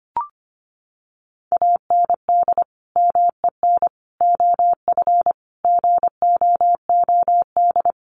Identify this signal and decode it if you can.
Morse code, heard as a keyed tone, about 25 wpm, 700 Hz. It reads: AND MEN OF GOOD
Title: Sentences from Top 1400 Words 25wpm